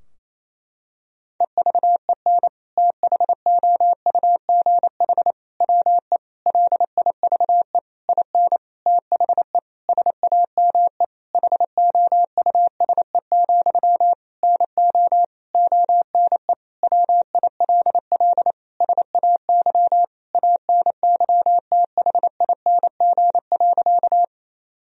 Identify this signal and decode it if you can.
Morse code, heard as a keyed tone, about 27 wpm, 700 Hz. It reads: EVEN THOUGH WE LIVE IN THE SAME HOUSE, NO ONE WILL SAY ANYTHING.
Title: Sentences from Top 1000 Words - Review Part 2 - 28wpm